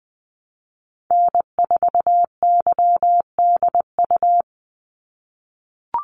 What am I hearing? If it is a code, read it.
Morse code, heard as a keyed tone, about 20 wpm, 700 Hz. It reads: N4YDU
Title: Call Signs - 20wpm